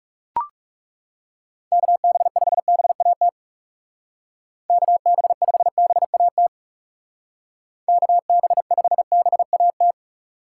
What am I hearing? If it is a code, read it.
Morse code, heard as a keyed tone, about 38 wpm, 700 Hz. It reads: KB5BAT KB5BAT KB5BAT
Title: Speed Racing - USA Call Signs 30wpm